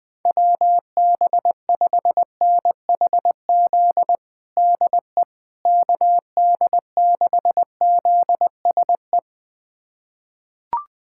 Morse code, heard as a keyed tone, about 20 wpm, 700 Hz. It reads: WB5NHZ DE KD6ZSE
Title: USA Callsign DE Callsign 20wpm